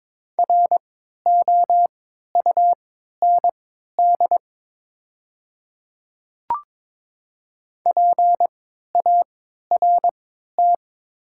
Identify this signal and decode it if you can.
Morse code, heard as a keyed tone, about 22 wpm, spaced as for 12 wpm, 700 Hz. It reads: ROUND PART
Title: 3x Character Spacing - Top 200 Words - 22wpm